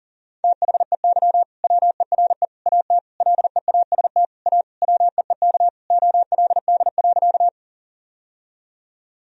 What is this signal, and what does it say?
Morse code, heard as a keyed tone, about 40 wpm, 700 Hz. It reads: THEY WERE AT LEAST A WEEK OLD.
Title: Sentences from Top 1000 Words - Review Part 2 - 40wpm